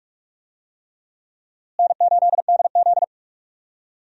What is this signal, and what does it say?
Morse code, heard as a keyed tone, about 46 wpm, 700 Hz. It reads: N8DZ
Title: USA Call Signs - 4 Characters - Repeated 3x - 45wpm